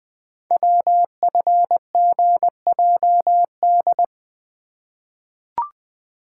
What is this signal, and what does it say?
Morse code, heard as a keyed tone, about 20 wpm, 700 Hz. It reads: WFGJD
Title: Code Groups - 5 Letter:Numbers - 20wpm